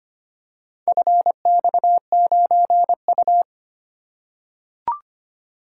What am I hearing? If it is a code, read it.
Morse code, heard as a keyed tone, about 25 wpm, 700 Hz. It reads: FX9U
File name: practice / 25wpm / Four Letter-Number Combinations - 25wpm